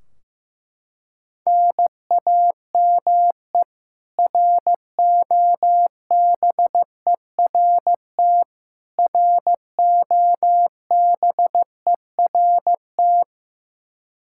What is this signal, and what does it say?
Morse code, heard as a keyed tone, about 15 wpm, 700 Hz. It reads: NAME ROBERT ROBERT
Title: Practice QSO - Exchange 1 - Beginning - 15wpm